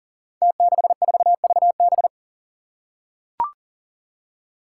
Morse code, heard as a keyed tone, about 39 wpm, 700 Hz. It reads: T64VB
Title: Five Challenging Letter-Number Combinations 40wpm